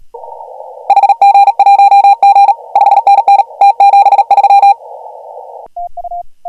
Morse code, heard as a keyed tone, about 38 wpm, 800 Hz. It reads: HG1G 5NN T73
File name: AUDIO2019 / HG1G CW